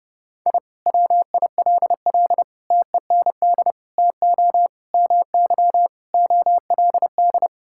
Morse code, heard as a keyed tone, about 32 wpm, 700 Hz. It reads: I WILL TEND TO MY OLD
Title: Sentences from Top 1000 Words - Review Part 5 - 30wpm